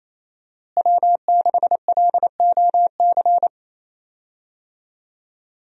Morse code, heard as a keyed tone, about 28 wpm, 700 Hz. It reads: W6LOC
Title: USA Call Signs - 5 Characters - Repeated 2x - 28wpm